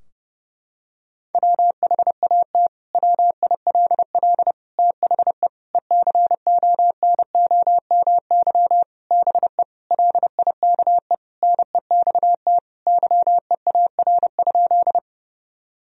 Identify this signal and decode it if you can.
Morse code, heard as a keyed tone, about 32 wpm, 700 Hz. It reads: WHAT WILL THE ECONOMY BE LIKE NEXT YEAR?
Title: Sentences from Top 1000 Words - Review Part 3 - 30wpm